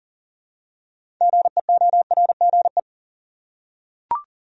Morse code, heard as a keyed tone, about 40 wpm, 700 Hz. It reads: GEORGE
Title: USA Common Names - 40wpm